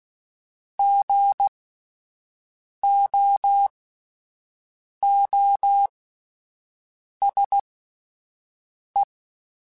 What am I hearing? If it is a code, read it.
Morse code, heard as a keyed tone, about 16 wpm, spaced as for 5 wpm, 800 Hz. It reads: GOOSE
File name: Spaced Alphas / Spaced Alphas mp3s / Gs sp @10